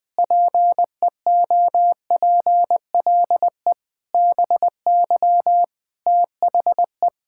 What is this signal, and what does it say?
Morse code, heard as a keyed tone, about 20 wpm, 700 Hz. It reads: PEOPLE BY THE
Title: Sentences with 9-Word Phrases - Spoken Beginning - 20wpm